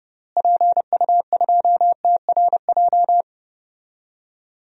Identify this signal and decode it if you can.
Morse code, heard as a keyed tone, about 30 wpm, 700 Hz. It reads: PU2TRJ
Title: DXCC-100 Call Signs - 6 Characters - Repeated 3x - 30wpm